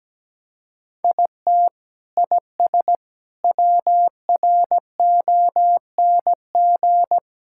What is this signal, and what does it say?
Morse code, heard as a keyed tone, about 17 wpm, 700 Hz. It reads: IT IS WRONG
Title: Sentences from Top 1000 Words - Review Part 5 - 17wpm